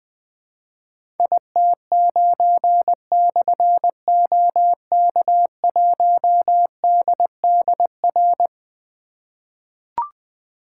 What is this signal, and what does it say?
Morse code, heard as a keyed tone, about 20 wpm, 700 Hz. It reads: IT9/OK1DDR
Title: Difficult Call Signs 20wpm